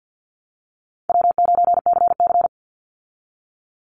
Morse code, heard as a keyed tone, about 50 wpm, 700 Hz. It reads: W9CC
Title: Easy Call Signs Repeated 3x - 50wpm